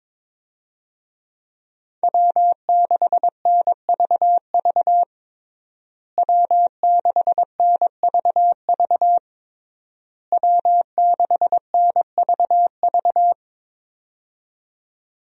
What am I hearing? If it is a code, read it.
Morse code, heard as a keyed tone, about 22 wpm, 700 Hz. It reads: W6NVV W6NVV W6NVV
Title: USA Call Signs - 5 Characters - Repeated 3x - 22wpm